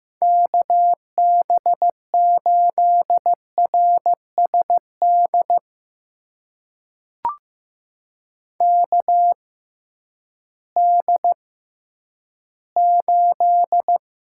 Morse code, heard as a keyed tone, about 15 wpm, spaced as for 10 wpm, 700 Hz. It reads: KB8RSD KD8
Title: USA Call Signs - 6 Characters - 6x Character Spacing - 15wpm